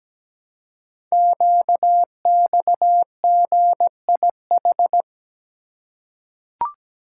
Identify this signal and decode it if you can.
Morse code, heard as a keyed tone, about 17 wpm, 700 Hz. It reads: QXGIH
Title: Code Groups - 5 Letters - 17wpm